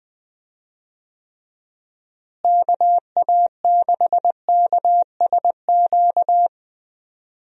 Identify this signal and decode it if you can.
Morse code, heard as a keyed tone, about 20 wpm, 700 Hz. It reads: KA6KSQ
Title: USA Call Signs - 6 Characters - Repeated 3x - 20wpm